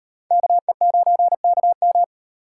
Morse code, heard as a keyed tone, about 38 wpm, 700 Hz. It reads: KE9KM